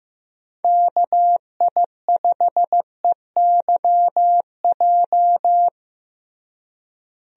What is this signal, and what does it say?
Morse code, heard as a keyed tone, about 15 wpm, 700 Hz. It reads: KI5EYJ
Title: USA Call Signs - 6 Characters - Repeated 2x - 15wpm